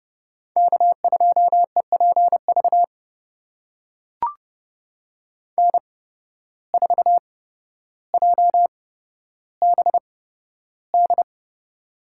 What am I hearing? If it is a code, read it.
Morse code, heard as a keyed tone, about 30 wpm, spaced as for 13 wpm, 700 Hz. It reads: K2EPV N4JBD
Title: USA Call Signs - 5 Characters - 8x Character Spacing - 30wpm